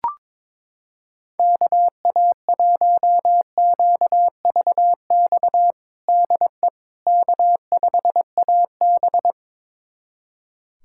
Morse code, heard as a keyed tone, about 22 wpm, 700 Hz. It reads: KA1QVX DE K5AB
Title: USA Callsign DE Callsign 22wpm